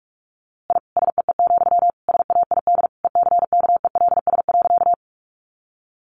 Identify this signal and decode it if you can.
Morse code, heard as a keyed tone, about 44 wpm, 700 Hz. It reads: I SEE, SAID ECKELS.
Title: A Sound Of Thunder 45wpm